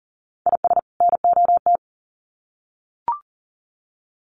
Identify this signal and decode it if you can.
Morse code, heard as a keyed tone, about 40 wpm, 700 Hz. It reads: IS NOT
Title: Sets of 2 Words 40wpm